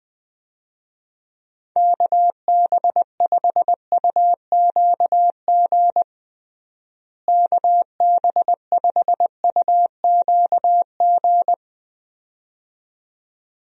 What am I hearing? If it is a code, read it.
Morse code, heard as a keyed tone, about 20 wpm, 700 Hz. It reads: KB5UQG KB5UQG
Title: USA Call Signs - 6 Characters - Repeated 2x - 20wpm